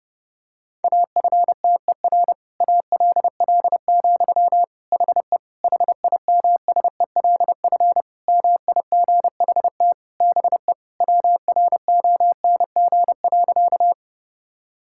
Morse code, heard as a keyed tone, about 32 wpm, 700 Hz. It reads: AFTER ALL, HE HIMSELF MIGHT BE WRONG.
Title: Sentences from Top 1600 Words 30wpm